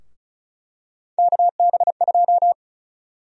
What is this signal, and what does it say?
Morse code, heard as a keyed tone, about 34 wpm, 700 Hz. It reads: KB2
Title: USA Call Sign Prefixes - 35wpm